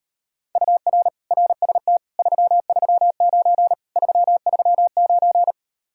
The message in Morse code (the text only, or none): UR RST 339 339